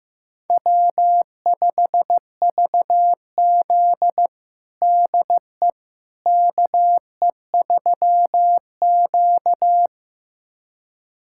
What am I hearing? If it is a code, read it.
Morse code, heard as a keyed tone, about 15 wpm, 700 Hz. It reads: W5VZ DE KE3Q
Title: USA Callsign DE Callsign - 4 Characters - 15wpm